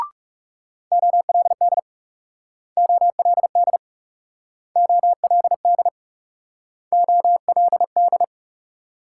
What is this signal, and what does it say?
Morse code, heard as a keyed tone, about 40 wpm, 700 Hz. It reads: OLD OLD OLD OLD